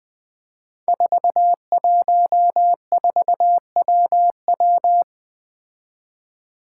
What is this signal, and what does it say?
Morse code, heard as a keyed tone, about 20 wpm, 700 Hz. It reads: 414WW
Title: Five Challenging Letter-Number Combinations 20wpm